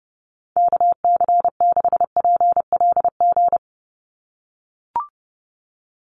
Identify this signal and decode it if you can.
Morse code, heard as a keyed tone, about 30 wpm, 700 Hz. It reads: KC6PLG
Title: USA Call Signs 30wpm